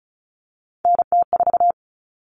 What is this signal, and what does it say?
Morse code, heard as a keyed tone, about 34 wpm, 700 Hz. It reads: NT4
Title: Common Cut Numbers - 3-Digit 35wpm